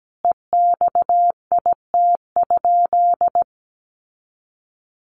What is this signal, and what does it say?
Morse code, heard as a keyed tone, about 17 wpm, 700 Hz. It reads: EXIT?
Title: Sentences from Top 1100 Words 17wpm